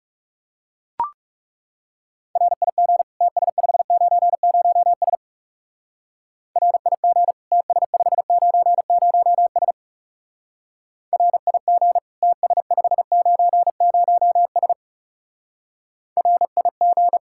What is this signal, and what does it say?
Morse code, heard as a keyed tone, about 36 wpm, 700 Hz. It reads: RIG TS590S RIG TS590S RIG TS590S RIG
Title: Speed-Racing Common QSO Phrases - 30wpm